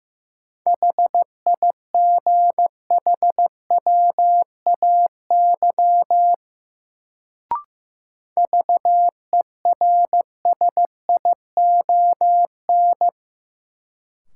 Morse code, heard as a keyed tone, about 15 wpm, 700 Hz. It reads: HIGHWAY VERSION
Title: Sentences from Top 1400 Words - New Word Drill - 15wpm